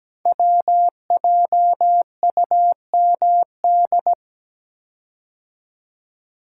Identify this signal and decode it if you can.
Morse code, heard as a keyed tone, about 17 wpm, 700 Hz. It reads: WJUMD
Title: Code Groups - 5 Letters - 17wpm